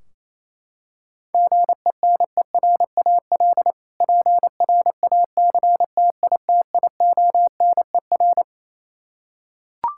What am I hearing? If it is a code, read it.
Morse code, heard as a keyed tone, about 28 wpm, 700 Hz. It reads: GENERAL PRACTITIONER
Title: K1USN SST QSO Practice 28wpm